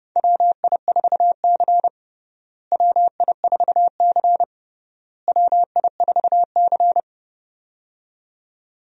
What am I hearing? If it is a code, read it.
Morse code, heard as a keyed tone, about 32 wpm, 700 Hz. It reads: WI4C WI4C WI4C
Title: USA Call Signs - 4 Characters - Repeated 3x - 30wpm